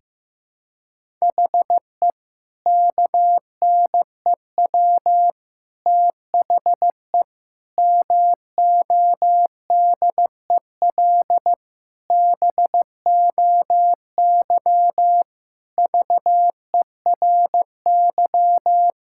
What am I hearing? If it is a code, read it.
Morse code, heard as a keyed tone, about 16 wpm, 700 Hz. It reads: HE KNEW THE MODEL BOY VERY
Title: Sentences from Top 1000 Words - Review Part 5 - 15wpm